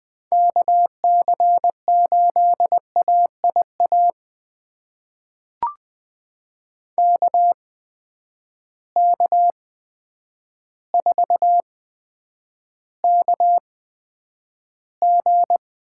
Morse code, heard as a keyed tone, about 20 wpm, spaced as for 9 wpm, 700 Hz. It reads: KC8AIA KK4KG
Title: USA Call Signs - 6 Characters - 8x Character Spacing - 20wpm